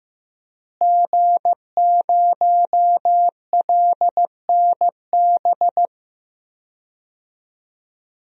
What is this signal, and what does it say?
Morse code, heard as a keyed tone, about 16 wpm, 700 Hz. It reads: G0LNB
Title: DXCC-100 Call Signs - 5 Characters - Repeated 3x - 15wpm